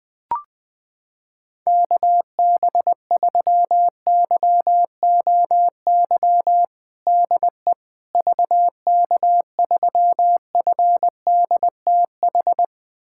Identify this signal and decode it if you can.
Morse code, heard as a keyed tone, about 21 wpm, 700 Hz. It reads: KB3YOY DE VK3FDTH
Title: USA Callsign DE Callsign 20wpm